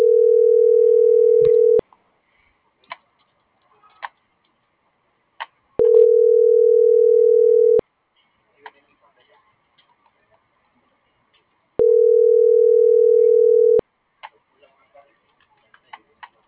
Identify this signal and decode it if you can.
Ringback tone (North America), 440 + 480 Hz.
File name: CENTRAL GENERAL / out karla.sky